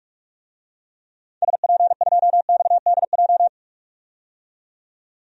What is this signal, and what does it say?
Morse code, heard as a keyed tone, about 43 wpm, 700 Hz. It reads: SP2XDJ
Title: DXCC-100 Call Signs - 6 Characters - 45wpm